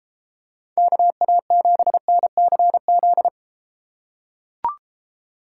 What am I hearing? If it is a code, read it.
Morse code, heard as a keyed tone, about 32 wpm, 700 Hz. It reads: KA7NCZ